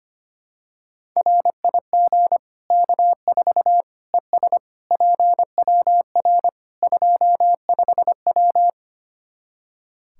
Morse code, heard as a keyed tone, about 25 wpm, 700 Hz. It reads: RIG K4 ES PWR 25W
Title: Practice QSO - Exchange 2 - Info - 25wpm